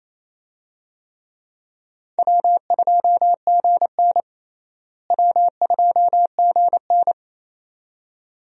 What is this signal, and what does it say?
Morse code, heard as a keyed tone, about 27 wpm, 700 Hz. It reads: W2GN W2GN
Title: USA Call Signs - 4 Characters - Repeated 2x - 28wpm